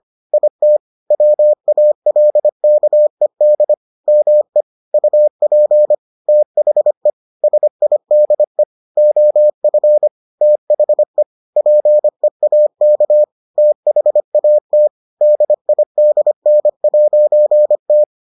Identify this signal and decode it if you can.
Morse code, heard as a keyed tone, about 25 wpm, 600 Hz. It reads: IT WALKED ME UP THE SIDE OF THE PEAK THAT DIDN'T